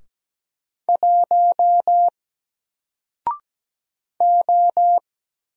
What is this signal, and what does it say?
Morse code, heard as a keyed tone, about 17 wpm, 700 Hz. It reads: 1 O
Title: Single Letter-Number Alt 17wpm